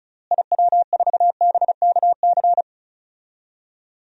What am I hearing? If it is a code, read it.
Morse code, heard as a keyed tone, about 34 wpm, 700 Hz. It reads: IW4BKC
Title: DXCC-100 Call Signs - 6 Characters - Repeated 2x - 35wpm